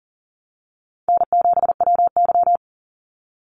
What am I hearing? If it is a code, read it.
Morse code, heard as a keyed tone, about 40 wpm, 700 Hz. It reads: N7WY
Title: Call Signs Repeated 2x - 40wpm